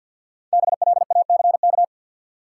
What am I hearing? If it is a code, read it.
Morse code, heard as a keyed tone, about 49 wpm, 700 Hz. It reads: BLACK
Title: Sentences from Top 400 Words - New Word Drill - 50wpm